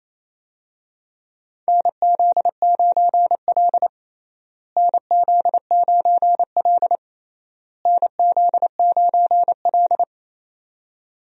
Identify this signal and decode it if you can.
Morse code, heard as a keyed tone, about 28 wpm, 700 Hz. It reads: NZ9L NZ9L NZ9L
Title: USA Call Signs - 4 Characters - Repeated 3x - 28wpm